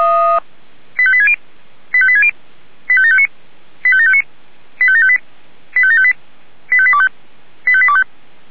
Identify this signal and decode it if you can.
Five-tone selective call (ZVEI), heard as quick runs of five tones, tones of about 70 ms. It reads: 85860 85860 85759 85759 85758 85758 85825 85825